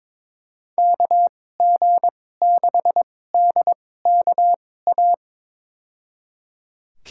Morse code, heard as a keyed tone, about 21 wpm, spaced as for 16 wpm, 700 Hz. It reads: KG6DKA
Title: USA Call Signs - 6 Characters - 2x Character Spacing - 22wpm